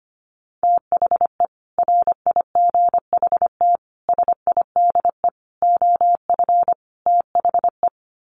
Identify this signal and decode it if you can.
Morse code, heard as a keyed tone, about 25 wpm, 700 Hz. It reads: THE RIGHT SIDE OF THE
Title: Sets of 5 Words Encore 25wpm